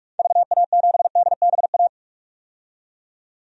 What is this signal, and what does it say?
Morse code, heard as a keyed tone, about 46 wpm, 700 Hz. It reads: VA7DBA
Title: DXCC-100 Call Signs - 6 Characters - 45wpm